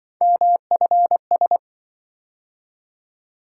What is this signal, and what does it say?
Morse code, heard as a keyed tone, about 25 wpm, 700 Hz. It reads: MFS
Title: Speed-Racing - Three Letter-Number Combinations - 20wpm